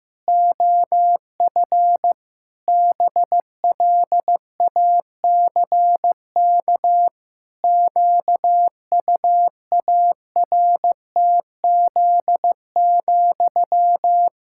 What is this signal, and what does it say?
Morse code, heard as a keyed tone, about 15 wpm, 700 Hz. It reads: OF BLACK QUARTZ,
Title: Pangram Challenge v2 - 15wpm